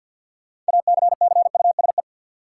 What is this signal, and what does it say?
Morse code, heard as a keyed tone, about 52 wpm, 700 Hz. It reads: ACCUSE
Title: Six Letter Words - 50wpm